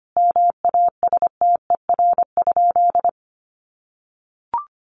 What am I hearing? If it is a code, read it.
Morse code, heard as a keyed tone, about 25 wpm, 700 Hz. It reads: MASTER?
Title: Sentences from Top 1300 Words 25wpm